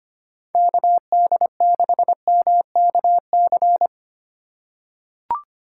Morse code, heard as a keyed tone, about 25 wpm, 700 Hz. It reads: KD6MKC
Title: USA Call Signs - 6 Characters - 6x Character Spacing - 25wpm